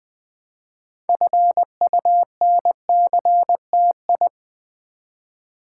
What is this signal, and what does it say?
Morse code, heard as a keyed tone, about 20 wpm, 700 Hz. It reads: FUNCTI
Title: Build A Word - 2 Letters At A Time - 8-Letter Words - 20wpm